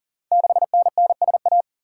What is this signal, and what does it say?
Morse code, heard as a keyed tone, about 38 wpm, 690 Hz. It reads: 6NNSA
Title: Code Groups - 5 Letter:Numbers - 40wpm